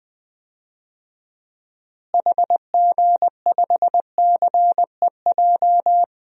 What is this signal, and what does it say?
Morse code, heard as a keyed tone, about 20 wpm, 700 Hz. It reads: HG5CEJ
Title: DXCC-100 Call Signs - 6 Characters - Repeated 3x - 20wpm